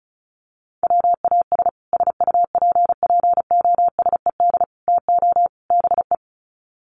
Morse code, heard as a keyed tone, about 34 wpm, 700 Hz. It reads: WAS SUPPOSED TO BE
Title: Sets of 4 Words 35wpm